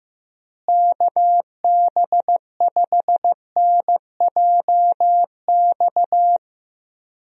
Morse code, heard as a keyed tone, about 15 wpm, 700 Hz. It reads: KB5NJX
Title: USA Call Signs - 6 Characters - Repeated 2x - 15wpm